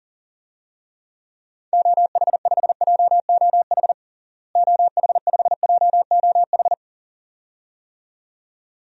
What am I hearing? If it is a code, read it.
Morse code, heard as a keyed tone, about 40 wpm, 700 Hz. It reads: OH5JOH OH5JOH
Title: DXCC-100 Call Signs - 6 Characters - Repeated 2x - 40wpm